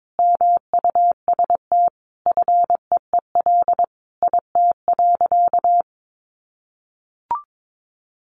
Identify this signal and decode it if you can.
Morse code, heard as a keyed tone, about 22 wpm, 700 Hz. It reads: MUST FEEL IT.
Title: Sentences from Top 400 Words 22wpm